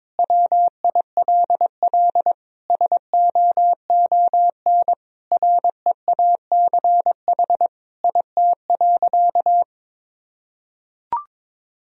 Morse code, heard as a keyed tone, about 23 wpm, 700 Hz. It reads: WILL SOON REACH IT.
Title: Sentences from Top 400 Words and Abbreviations 22wpm